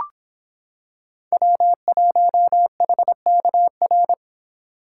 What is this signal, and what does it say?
Morse code, heard as a keyed tone, about 25 wpm, 700 Hz. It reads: W1HKR